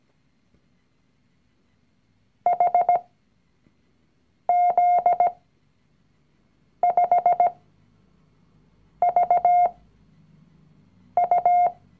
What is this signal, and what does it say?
Morse code, heard as a keyed tone, about 17 wpm, spaced as for 5 wpm, 710 Hz. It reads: HZ5VU
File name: CW-Online-Kurs / Anfänger / Lektion 019